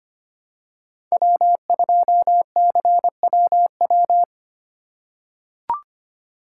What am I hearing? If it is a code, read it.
Morse code, heard as a keyed tone, about 25 wpm, 700 Hz. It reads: W2CWW
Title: USA Call Signs - 5 Characters - 3x Character Spacing - 25wpm